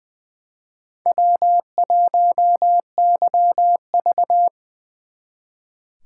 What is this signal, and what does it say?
Morse code, heard as a keyed tone, about 24 wpm, 700 Hz. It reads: W1YV
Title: Speed Racing - USA 1x2 Call Signs 20wpm